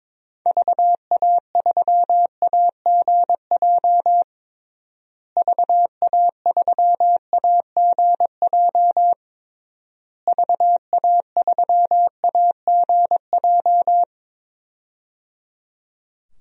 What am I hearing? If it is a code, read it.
Morse code, heard as a keyed tone, about 22 wpm, 700 Hz. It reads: VA3AGJ VA3AGJ VA3AGJ
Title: DXCC-100 Call Signs - 6 Characters - Repeated 3x - 22wpm